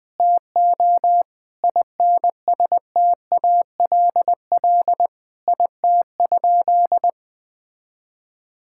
Morse code, heard as a keyed tone, about 20 wpm, 700 Hz. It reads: TO INSTALL IT?
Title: Sentences from Top 1600 Words 20wpm